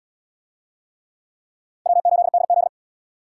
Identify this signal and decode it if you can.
Morse code, heard as a keyed tone, about 77 wpm, 700 Hz. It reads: W8DZ